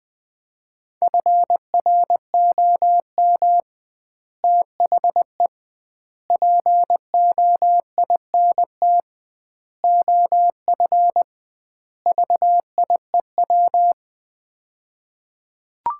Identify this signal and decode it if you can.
Morse code, heard as a keyed tone, about 20 wpm, 700 Hz. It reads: FROM THE POINT OF VIEW
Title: 2x Word-Spacing - Sets of 5 Words - 20wpm